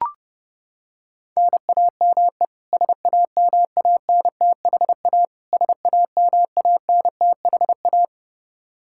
Morse code, heard as a keyed tone, about 30 wpm, 700 Hz. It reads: NAME SAMANTHA SAMANTHA
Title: Practice QSO - Exchange 1 - Beginning - 30wpm